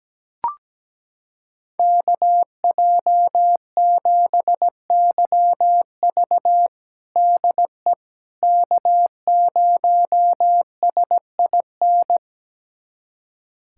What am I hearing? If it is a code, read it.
Morse code, heard as a keyed tone, about 17 wpm, 700 Hz. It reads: KJ7YV DE K0SIN
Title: USA Callsign DE Callsign - 5 Characters - 17wpm